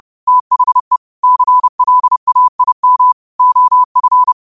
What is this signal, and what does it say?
Morse code, heard as a keyed tone, about 30 wpm, 1000 Hz. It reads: THE CLAIM OF
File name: qotdfiles / 30wpm